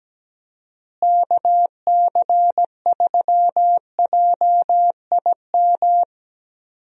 Morse code, heard as a keyed tone, about 17 wpm, 700 Hz. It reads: KC3JIM
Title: USA Call Signs - 6 Characters - Repeated 3x - 17wpm